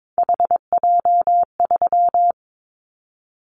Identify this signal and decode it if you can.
Morse code, heard as a keyed tone, about 22 wpm, 700 Hz. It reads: HJ3
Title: Callsign Prefixes for Top 100 DXCC - 22wpm